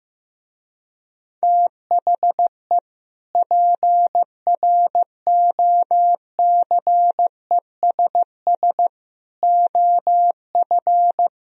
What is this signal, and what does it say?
Morse code, heard as a keyed tone, about 16 wpm, 700 Hz. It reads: THE PROCESS OF
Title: Speed-Racing - Sets of 3 Words 15wpm